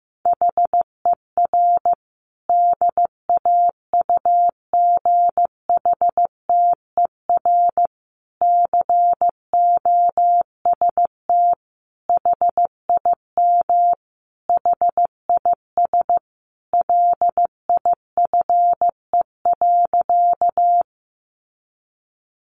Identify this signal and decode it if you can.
Morse code, heard as a keyed tone, about 15 wpm, 700 Hz. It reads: HER DAUGHTER COST HIM HIS LIFE.
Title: Sentences from Top 1500 Words 15wpm